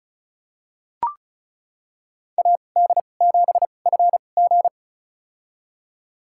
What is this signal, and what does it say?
Morse code, heard as a keyed tone, about 34 wpm, spaced as for 25 wpm, 700 Hz. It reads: AD7FG
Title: USA Call Signs - 2x Character Spacing - 35wpm